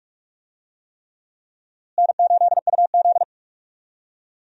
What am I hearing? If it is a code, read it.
Morse code, heard as a keyed tone, about 46 wpm, 700 Hz. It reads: N8UZ
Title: USA Call Signs - 4 Characters - Repeated 3x - 45wpm